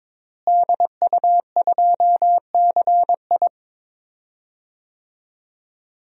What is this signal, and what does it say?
Morse code, heard as a keyed tone, about 22 wpm, 700 Hz. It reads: DU2CI
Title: DXCC-100 Call Signs - 5 Characters - Repeated 3x - 22wpm